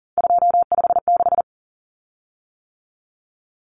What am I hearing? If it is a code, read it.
Morse code, heard as a keyed tone, about 40 wpm, 700 Hz. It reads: 256
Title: Common Cut Numbers - 3-Digit 40wpm